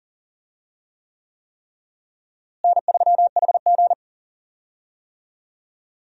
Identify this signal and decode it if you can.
Morse code, heard as a keyed tone, about 40 wpm, 700 Hz. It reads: N3HG
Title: USA Call Signs - 4 Characters - Repeated 3x - 40wpm